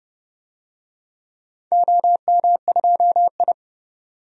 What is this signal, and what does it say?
Morse code, heard as a keyed tone, about 30 wpm, 700 Hz. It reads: OM2I
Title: DXCC-100 Call Signs - 4 Characters - 30wpm